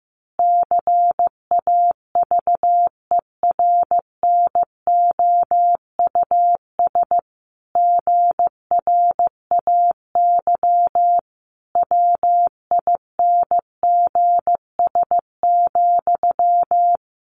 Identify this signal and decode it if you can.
Morse code, heard as a keyed tone, about 15 wpm, 700 Hz. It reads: CAVERNOUS GRAY WINGS,
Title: A Sound Of Thunder 15wpm